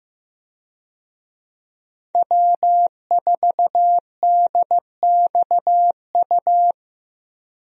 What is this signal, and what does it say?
Morse code, heard as a keyed tone, about 15 wpm, 700 Hz. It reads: W4DXU
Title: USA Call Signs - 5 Characters - 15wpm